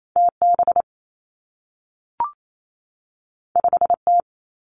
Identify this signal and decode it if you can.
Morse code, heard as a keyed tone, about 28 wpm, 700 Hz. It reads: T B 5 T
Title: Two Challenging Letter-Number Combinations 28wpm